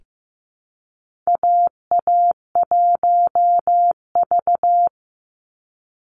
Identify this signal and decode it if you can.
Morse code, heard as a keyed tone, about 15 wpm, 700 Hz. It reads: AA1V
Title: Call Signs Repeated 3x - 15wpm